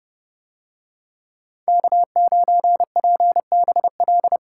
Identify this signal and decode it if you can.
Morse code, heard as a keyed tone, about 29 wpm, 700 Hz. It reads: K9PBL